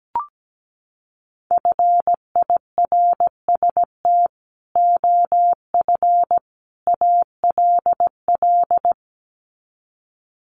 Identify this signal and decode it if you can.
Morse code, heard as a keyed tone, about 17 wpm, 700 Hz. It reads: FIRST OF ALL
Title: Sets of 3 Words 17wpm